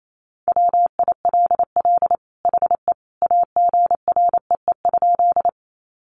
Morse code, heard as a keyed tone, about 28 wpm, 700 Hz. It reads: WILL HE AGREE?
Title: Sentences from Top 1100 Words 28wpm